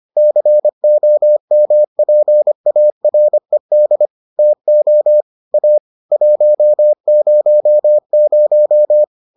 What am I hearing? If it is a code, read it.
Morse code, heard as a keyed tone, about 25 wpm, 600 Hz. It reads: COMPARED TO A 100